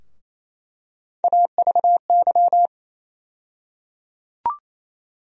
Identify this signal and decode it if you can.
Morse code, heard as a keyed tone, about 28 wpm, 700 Hz. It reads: AVY
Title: USA Call Sign Postfixes - 3 Letters 28wpm